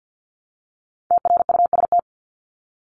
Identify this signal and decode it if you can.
Morse code, heard as a keyed tone, about 50 wpm, 700 Hz. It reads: TRUST
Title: Top 1751-2000 Words 50wpm